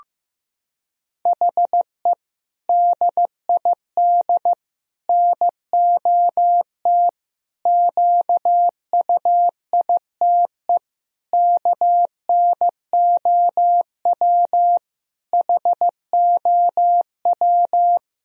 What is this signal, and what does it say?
Morse code, heard as a keyed tone, about 15 wpm, 700 Hz. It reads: HE DID NOT QUITE KNOW HOW
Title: Sentences from Top 1000 Words - Review Part 1 - 15wpm